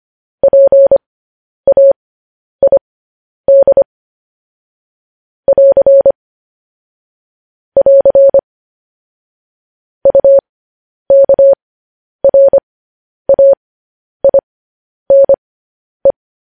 Morse code, heard as a keyed tone, about 25 wpm, spaced as for 10 wpm, 550 Hz. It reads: PAID <AR> <AR> UKRAINE
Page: news headlines in morse code at 10 wpm